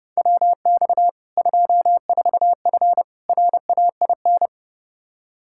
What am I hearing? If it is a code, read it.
Morse code, heard as a keyed tone, about 30 wpm, 700 Hz. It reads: WX 24F RAIN
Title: Practice QSO - Exchange 2 - Info - 30wpm